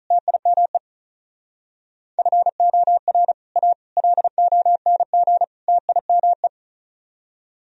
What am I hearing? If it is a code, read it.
Morse code, heard as a keyed tone, about 45 wpm, 700 Hz. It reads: TIME FOR A LONG TIME